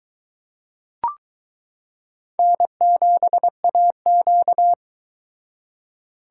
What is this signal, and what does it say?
Morse code, heard as a keyed tone, about 22 wpm, 700 Hz. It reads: N7AQ